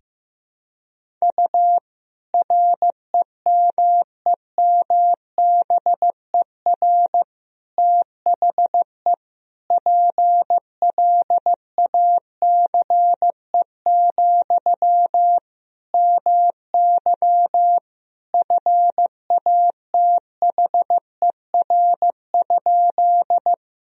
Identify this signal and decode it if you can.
Morse code, heard as a keyed tone, about 15 wpm, 700 Hz. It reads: U REMEMBER THE PLACE, MY FATHER?
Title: Sentences from Top 400 Words and Abbreviations 15wpm